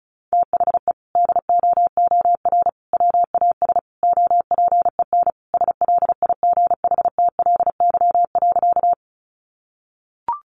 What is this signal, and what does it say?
Morse code, heard as a keyed tone, about 34 wpm, 700 Hz. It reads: THE DOOR WAS OPEN SLIGHTLY.
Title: Sentences from Top 1300 Words 35wpm